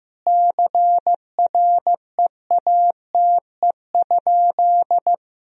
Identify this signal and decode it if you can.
Morse code, heard as a keyed tone, about 15 wpm, 700 Hz. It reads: CREATE?
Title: Sentences from Top 1000 Words - Review Part 3 - 15wpm